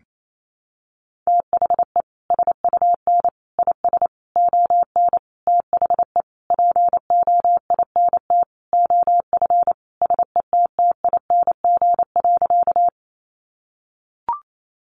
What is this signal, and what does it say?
Morse code, heard as a keyed tone, about 28 wpm, 700 Hz. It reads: THE SUN IS ON THE POINT OF SETTING.
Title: Sentences from Top 1300 Words 28wpm